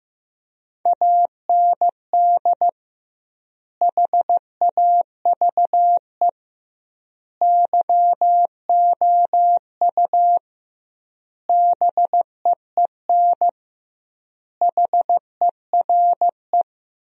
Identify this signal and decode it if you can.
Morse code, heard as a keyed tone, about 15 wpm, 700 Hz. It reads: AND HAVE YOU BEEN HERE
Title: 2x Word-Spacing - Sentences from Top 200 Words - 15wpm